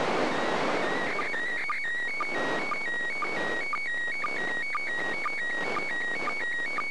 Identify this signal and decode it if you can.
SSTV (slow-scan television), PD 120.
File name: 2020.10.02 ISS